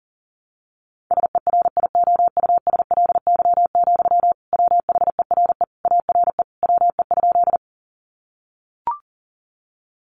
Morse code, heard as a keyed tone, about 40 wpm, 700 Hz. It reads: SERIOUSLY, WHERE ARE WE?
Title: Sentences from Top 1200 Words 40wpm